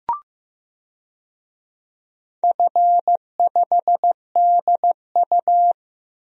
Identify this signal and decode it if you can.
Morse code, heard as a keyed tone, about 15 wpm, 700 Hz. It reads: F5DU
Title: DXCC-100 Call Signs - 4 Characters - 15wpm